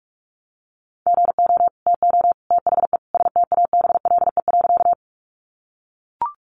Morse code, heard as a keyed tone, about 44 wpm, 700 Hz. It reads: GO TO THE STABLE.
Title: Sentences from Top 1500 Words 45wpm